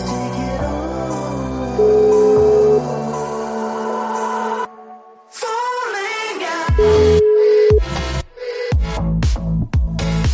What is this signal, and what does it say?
Ringback tone (Europe, 425 Hz), with noise.